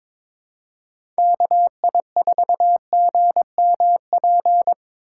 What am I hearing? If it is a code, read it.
Morse code, heard as a keyed tone, about 26 wpm, 700 Hz. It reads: KI4GMP
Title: Speed Racing - USA Call Signs 22wpm